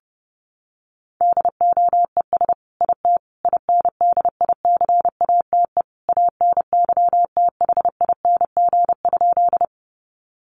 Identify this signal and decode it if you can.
Morse code, heard as a keyed tone, about 30 wpm, 700 Hz. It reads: DOES IT INDICATE ANYTHING?
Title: Sentences from Top 1300 Words 30wpm